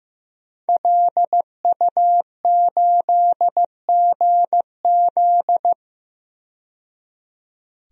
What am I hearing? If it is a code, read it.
Morse code, heard as a keyed tone, about 15 wpm, 700 Hz. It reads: LU8GZ
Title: DXCC-100 Call Signs - 5 Characters - 15wpm